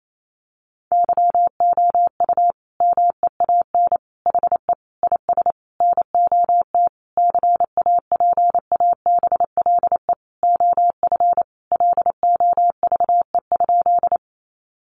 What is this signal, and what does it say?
Morse code, heard as a keyed tone, about 28 wpm, 700 Hz. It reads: YOU MEAN HE IS NOT CAPABLE OF LOVE?
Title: Sentences from Top 1200 Words 28wpm